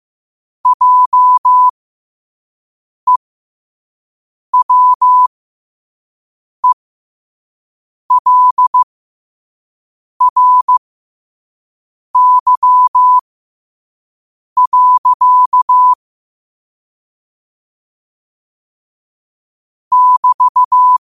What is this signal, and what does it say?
Morse code, heard as a keyed tone, about 15 wpm, spaced as for 5 wpm, 1000 Hz. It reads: JEWELRY. <BT>
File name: qotdfiles / 5wpm